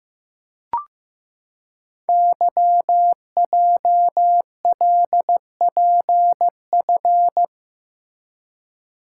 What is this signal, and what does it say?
Morse code, heard as a keyed tone, about 15 wpm, 700 Hz. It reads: YJLPF
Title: Code Groups - 5 Letters - 15wpm